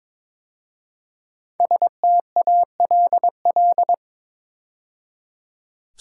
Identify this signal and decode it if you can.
Morse code, heard as a keyed tone, about 22 wpm, 700 Hz. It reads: STALL
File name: 5 Letter Words - First 3 Letters Given - 22wpm